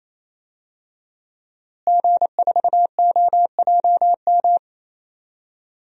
Morse code, heard as a keyed tone, about 27 wpm, 700 Hz. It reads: G4OJM
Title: DXCC-100 Call Signs - 5 Characters - Repeated 3x - 28wpm